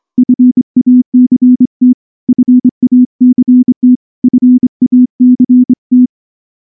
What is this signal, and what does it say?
Morse code, heard as a keyed tone, about 25 wpm, 250 Hz. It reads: FACT FACT FACT
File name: morse / morse-training-25WPM